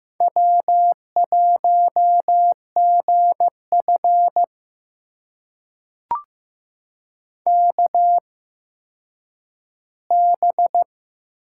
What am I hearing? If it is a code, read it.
Morse code, heard as a keyed tone, about 15 wpm, spaced as for 9 wpm, 700 Hz. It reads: W1GF KB
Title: USA Call Signs - 8x Character Spacing - 15wpm